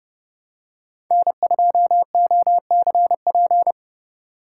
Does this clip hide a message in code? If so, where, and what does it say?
Yes: Morse code from 1.1 to 3.7 s, heard as a keyed tone, about 29 wpm, 700 Hz. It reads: N2OCP